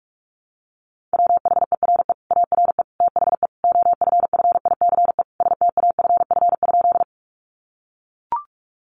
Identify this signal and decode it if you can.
Morse code, heard as a keyed tone, about 44 wpm, 700 Hz. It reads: WHERE ARE THE OFFICE STAFF?
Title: Sentences from Top 1200 Words 45wpm